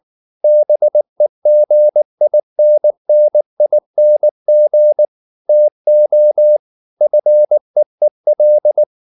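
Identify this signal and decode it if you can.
Morse code, heard as a keyed tone, about 19 wpm, 600 Hz. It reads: BEGINNING TO FEEL
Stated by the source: Morse code at 19 wpm from CW med Gnister, reading Ken Follett's novel Never in English.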